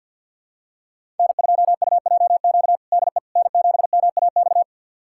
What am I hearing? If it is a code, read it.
Morse code, heard as a keyed tone, about 47 wpm, 700 Hz. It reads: N2UJQ DE N7MAX
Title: USA Callsign DE Callsign - 5 Characters - 50wpm